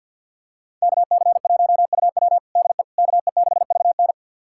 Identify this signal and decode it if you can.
Morse code, heard as a keyed tone, about 47 wpm, 700 Hz. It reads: KC1UW DE KE6VN
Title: USA Callsign DE Callsign - 5 Characters - 50wpm